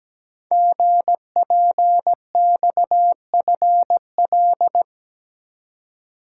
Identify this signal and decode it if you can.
Morse code, heard as a keyed tone, about 17 wpm, 700 Hz. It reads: GPXFL
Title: Code Groups - 5 Letters - 17wpm